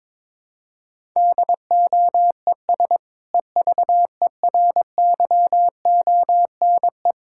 Morse code, heard as a keyed tone, about 22 wpm, 700 Hz. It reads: DOES EVERYONE
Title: Sentences from Top 1000 Words - Review Part 6 - 22wpm